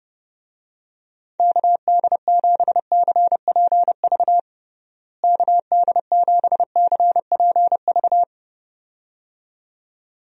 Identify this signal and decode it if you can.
Morse code, heard as a keyed tone, about 32 wpm, 700 Hz. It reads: KD7CPV KD7CPV
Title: USA Call Signs - 6 Characters - Repeated 2x - 30wpm